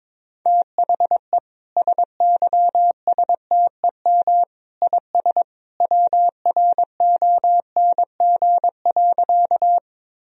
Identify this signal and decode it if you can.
Morse code, heard as a keyed tone, about 22 wpm, 700 Hz. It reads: THE SYSTEM IS WRONG.
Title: Sentences from Top 1000 Words - Review Part 3 - 22wpm